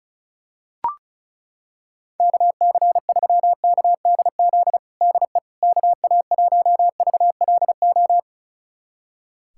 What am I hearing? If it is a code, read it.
Morse code, heard as a keyed tone, about 36 wpm, 700 Hz. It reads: KC3KDZ DE KA1VLO
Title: USA Callsign DE Callsign - 6 Characters - 35wpm